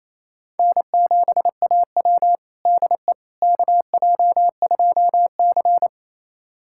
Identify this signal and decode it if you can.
Morse code, heard as a keyed tone, about 28 wpm, 700 Hz. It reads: N7AW DE KJ2C
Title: USA Callsign DE Callsign - 4 Characters - 28wpm